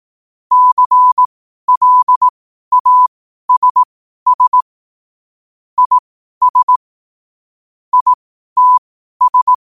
Morse code, heard as a keyed tone, about 18 wpm, spaced as for 12 wpm, 1000 Hz. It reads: CLASS IS ITS
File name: qotdfiles / 12wpm